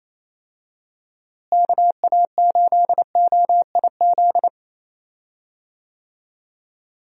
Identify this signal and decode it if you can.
Morse code, heard as a keyed tone, about 28 wpm, 700 Hz. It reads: KA8OIZ
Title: USA Call Signs - 6 Characters - Repeated 3x - 28wpm